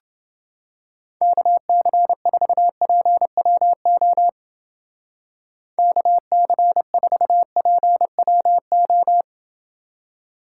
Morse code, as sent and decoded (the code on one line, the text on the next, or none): -.- -.-. ....- .--. .-- --- / -.- -.-. ....- .--. .-- ---
KC4PWO KC4PWO